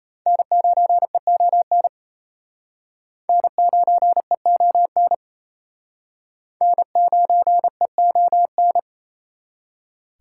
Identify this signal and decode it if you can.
Morse code, heard as a keyed tone, about 35 wpm, 700 Hz. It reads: N9EON N9EON N9EON
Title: Speed Racing - USA 1x3 Call Signs 28wpm